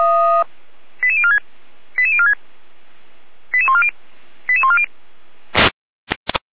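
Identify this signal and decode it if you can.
Five-tone selective call (ZVEI), heard as quick runs of five tones, tones of about 70 ms. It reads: 88046 88046 88140 88140
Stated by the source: Five-tone ZVEI selective call 88501 received by SDR, server of Feuerwehr Orlamünde.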